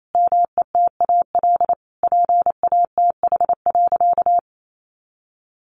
Morse code, heard as a keyed tone, about 28 wpm, 700 Hz. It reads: METAL PATH.
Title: A Sound Of Thunder 28wpm